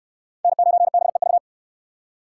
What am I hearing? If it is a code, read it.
Morse code, heard as a keyed tone, about 69 wpm, 700 Hz. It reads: N9BEV